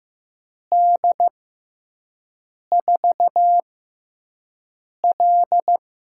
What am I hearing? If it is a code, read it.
Morse code, heard as a keyed tone, about 15 wpm, spaced as for 5 wpm, 700 Hz. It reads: D4L
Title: USA Call Signs - 6 Characters - 6x Character Spacing - 15wpm